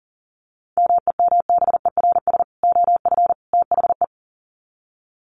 Morse code, heard as a keyed tone, about 40 wpm, 700 Hz. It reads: MEMBERS OF THE
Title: Sets of 3 Words 40wpm